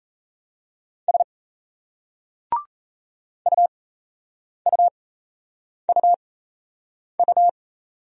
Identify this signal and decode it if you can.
Morse code, heard as a keyed tone, about 38 wpm, spaced as for 8 wpm, 700 Hz. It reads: S UUUU